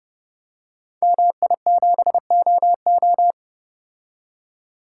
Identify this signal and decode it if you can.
Morse code, heard as a keyed tone, about 30 wpm, 700 Hz. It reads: MI7OO
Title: Code Groups - 5 Letter:Numbers - 30wpm